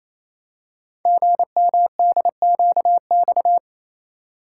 Morse code, heard as a keyed tone, about 27 wpm, 700 Hz. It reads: GMDQX
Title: Code Groups - 5 Letters - 28wpm